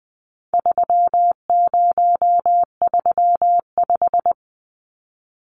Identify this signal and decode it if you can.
Morse code, heard as a keyed tone, about 20 wpm, 700 Hz. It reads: 3035
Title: Numbers - 4-Digit 20wpm